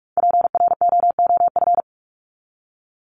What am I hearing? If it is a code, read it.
Morse code, heard as a keyed tone, about 44 wpm, 700 Hz. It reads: PROOF
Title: Top 1501-1750 Words 45wpm